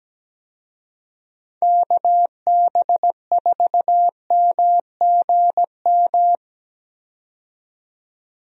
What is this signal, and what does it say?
Morse code, heard as a keyed tone, about 17 wpm, 700 Hz. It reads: KB4MGM
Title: USA Call Signs - 6 Characters - 17wpm